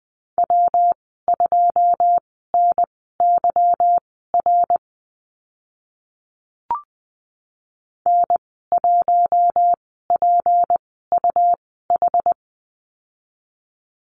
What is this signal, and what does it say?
Morse code, heard as a keyed tone, about 20 wpm, spaced as for 14 wpm, 700 Hz. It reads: W2NYR N1PUH
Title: 167 Lesson 27g - Number 7 - Call Signs 20wpm